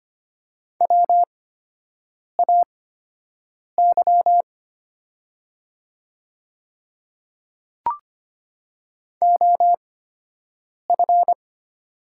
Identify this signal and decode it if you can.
Morse code, heard as a keyed tone, about 25 wpm, spaced as for 7 wpm, 700 Hz. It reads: WAY OF
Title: 8x Character Spacing - Top 100 Words - 25wpm